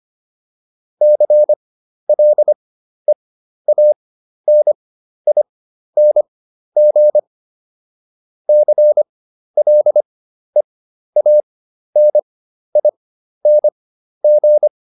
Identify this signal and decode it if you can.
Morse code, heard as a keyed tone, about 25 wpm, spaced as for 12 wpm, 600 Hz. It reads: CLEANING CLEANING